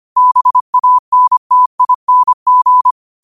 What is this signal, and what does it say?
Morse code, heard as a keyed tone, about 25 wpm, 1000 Hz. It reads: BANTING
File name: 25wpm